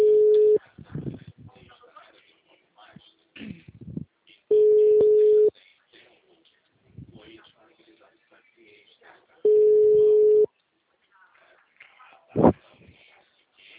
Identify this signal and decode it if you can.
Ringback tone (Europe, 425 Hz), with speech.